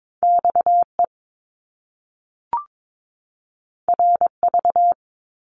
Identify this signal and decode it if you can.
Morse code, heard as a keyed tone, about 22 wpm, 700 Hz. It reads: XE RV
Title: Top Two Letter Combinations in Words 22wpm